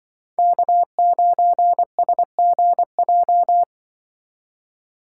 Morse code, heard as a keyed tone, about 22 wpm, 700 Hz. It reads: K9SGJ
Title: Speed Racing - USA 1x3 Call Signs 20wpm